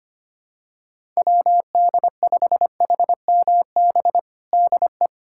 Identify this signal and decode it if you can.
Morse code, heard as a keyed tone, about 25 wpm, 700 Hz. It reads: WD5HMB DE
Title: USA Callsign DE Callsign 25wpm